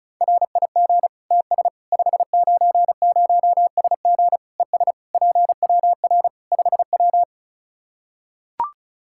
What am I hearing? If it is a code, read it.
Morse code, heard as a keyed tone, about 36 wpm, 700 Hz. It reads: RIG TS 590SG ES PWR 5W
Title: Practice QSO - Exchange 2 - Info - 35wpm